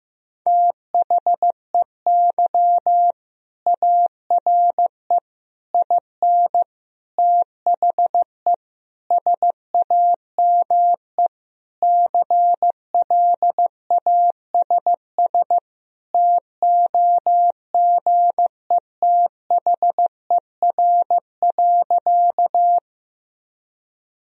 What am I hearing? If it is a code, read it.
Morse code, heard as a keyed tone, about 15 wpm, 700 Hz. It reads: THEY ARE IN THE SAME CLASS TOGETHER.
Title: Sentences from Top 1000 Words - Review Part 2 - 15wpm